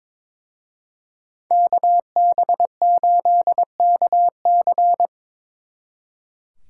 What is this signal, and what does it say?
Morse code, heard as a keyed tone, about 25 wpm, 700 Hz. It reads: KB8KC
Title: Speed Racing - USA 2x2 Call Signs 22wpm